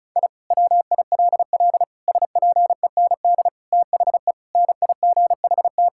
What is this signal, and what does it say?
Morse code, heard as a keyed tone, about 36 wpm, 700 Hz. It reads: I WILL SPEND THE NIGHT
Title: Sentences from Top 1000 Words - Review Part 5 - 35wpm